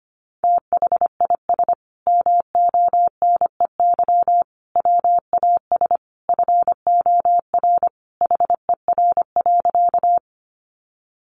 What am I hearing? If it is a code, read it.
Morse code, heard as a keyed tone, about 25 wpm, 700 Hz. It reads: THIS MONEY WAS FOR HER.
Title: Sentences from Top 400 Words 25wpm